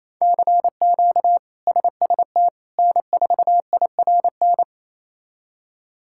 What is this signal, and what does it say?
Morse code, heard as a keyed tone, about 28 wpm, 700 Hz. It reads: CQ SST N4IRN
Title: K1USN SST QSO Practice - 28wpm